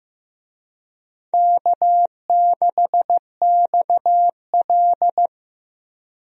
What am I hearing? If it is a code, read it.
Morse code, heard as a keyed tone, about 15 wpm, 700 Hz. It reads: K6XL
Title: USA Call Signs - 4 Characters - Repeated 2x - 15wpm